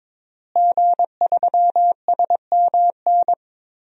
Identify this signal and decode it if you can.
Morse code, heard as a keyed tone, about 22 wpm, 700 Hz. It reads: G3SMN
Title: DXCC-100 Call Signs - 5 Characters - Repeated 3x - 22wpm